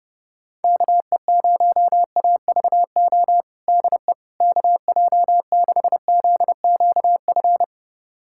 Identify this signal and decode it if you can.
Morse code, heard as a keyed tone, about 30 wpm, 700 Hz. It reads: KE0AVO DE KJ6ZQF
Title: USA Callsign DE Callsign 30wpm